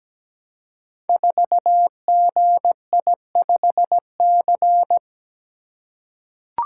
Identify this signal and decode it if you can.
Morse code, heard as a keyed tone, about 17 wpm, 700 Hz. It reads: 4GI5C
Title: Code Groups - 5 Letter:Numbers - 17wpm